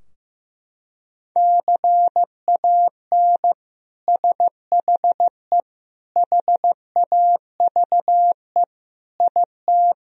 Morse code, heard as a keyed tone, about 16 wpm, 700 Hz. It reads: CAN SHE HAVE IT
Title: Sentences from Top 1000 Words - Review Part 4 - 15wpm